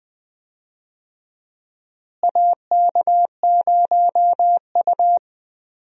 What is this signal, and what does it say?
Morse code, heard as a keyed tone, about 21 wpm, 700 Hz. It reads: AK0U
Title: USA Call Signs - 4 Characters - Repeated 3x - 20wpm